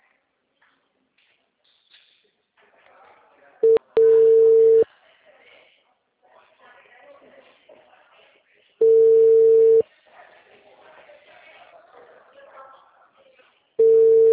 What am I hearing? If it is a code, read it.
Ringback tone (China, 450 Hz), with speech.